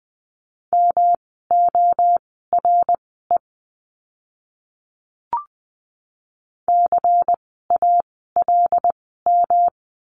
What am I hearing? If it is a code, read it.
Morse code, heard as a keyed tone, about 20 wpm, spaced as for 14 wpm, 700 Hz. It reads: MORE CALM
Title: 108 Lesson 18f - Letter m - Words 20wpm